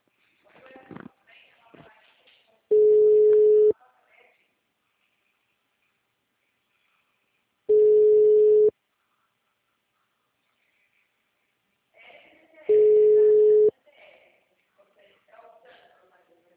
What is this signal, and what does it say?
Ringback tone (Europe, 425 Hz), with speech.
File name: out- -2015- - - .65181